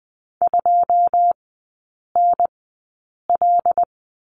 Morse code, heard as a keyed tone, about 20 wpm, spaced as for 8 wpm, 690 Hz. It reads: 2NL
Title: 172 Lesson 28e - Number 9 - Three Characters 20wpm